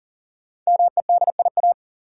Morse code, heard as a keyed tone, about 40 wpm, 700 Hz. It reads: MEDIA
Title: Sentences from Top 1000 Words - Review Part 4 - New Word Drill - 40wpm